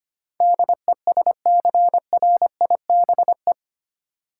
Morse code, heard as a keyed tone, about 25 wpm, 700 Hz. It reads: DESCRIBE
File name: 8 Letter Words - First 3 Letters Given - 25wpm